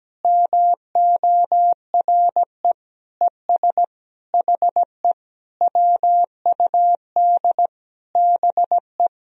Morse code, heard as a keyed tone, about 17 wpm, 700 Hz. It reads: MORE ES HE WUD BE
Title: Sentences from Top 300 Words and Abbreviations 17wpm